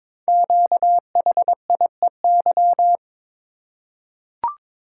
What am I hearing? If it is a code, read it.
Morse code, heard as a keyed tone, about 22 wpm, 700 Hz. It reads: QHIEY
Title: Code Groups - 5 Letters - 22wpm